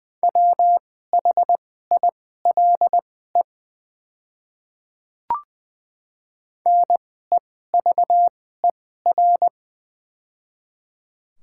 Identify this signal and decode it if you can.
Morse code, heard as a keyed tone, about 20 wpm, spaced as for 14 wpm, 700 Hz. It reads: WHILE NEVER
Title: 2x Character Spacing - Top 300 Words - 20wpm